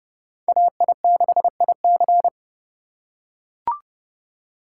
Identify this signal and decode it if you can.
Morse code, heard as a keyed tone, about 28 wpm, 700 Hz. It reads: AI6IC